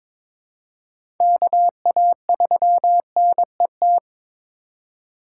Morse code, heard as a keyed tone, about 22 wpm, 700 Hz. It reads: KA3NET
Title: USA Call Signs - 6 Characters - 3x Character Spacing - 22wpm